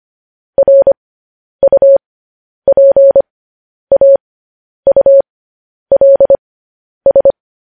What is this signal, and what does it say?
Morse code, heard as a keyed tone, about 25 wpm, spaced as for 10 wpm, 550 Hz. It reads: RUPAULS
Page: news headlines in morse code at 10 wpm